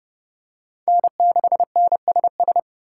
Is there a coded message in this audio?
Yes: Morse code from 0.9 to 2.6 s, heard as a keyed tone, about 29 wpm, 700 Hz. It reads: N6NSS